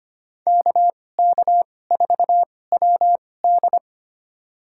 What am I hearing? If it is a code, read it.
Morse code, heard as a keyed tone, about 25 wpm, spaced as for 18 wpm, 700 Hz. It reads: KK4WD
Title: USA Call Signs - 5 Characters - 2x Character Spacing - 25wpm